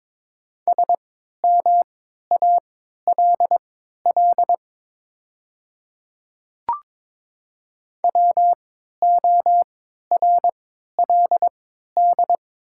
Morse code, heard as a keyed tone, about 22 wpm, spaced as for 13 wpm, 700 Hz. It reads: SMALL WORLD
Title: 3x Character Spacing - Top 200 Words - 22wpm